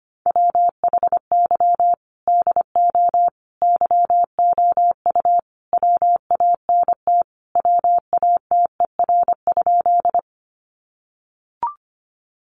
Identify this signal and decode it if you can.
Morse code, heard as a keyed tone, about 25 wpm, 700 Hz. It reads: WHY DO YOU WANT WATER?
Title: Sentences from Top 200 Words 25wpm